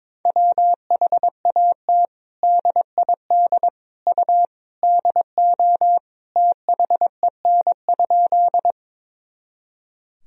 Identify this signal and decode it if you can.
Morse code, heard as a keyed tone, about 22 wpm, 700 Hz. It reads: WHAT DID U DO THEN?
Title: Sentences from Top 100 Words and Abbreviations 22wpm